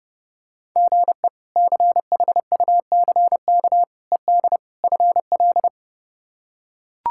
Morse code, heard as a keyed tone, about 30 wpm, 700 Hz. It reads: GE CHUCK ED FL
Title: K1USN SST QSO Practice - 30wpm